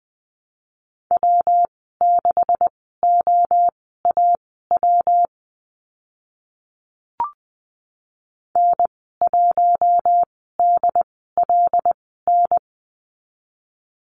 Morse code, heard as a keyed tone, about 20 wpm, spaced as for 14 wpm, 700 Hz. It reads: W6OAW N1DLN
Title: 167 Lesson 27g - Number 7 - Call Signs 20wpm